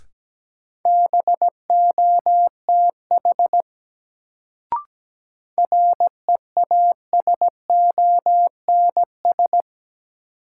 Morse code, heard as a keyed tone, about 17 wpm, 700 Hz. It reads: BOTH REASONS
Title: Sets of 4 Words - New Word Drill - 17wpm